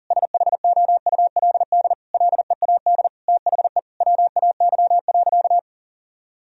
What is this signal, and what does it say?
Morse code, heard as a keyed tone, about 37 wpm, 700 Hz. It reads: SHOULD LEAD THE WAY.
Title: Sentences from Top 1000 Words - Review Part 2 - 40wpm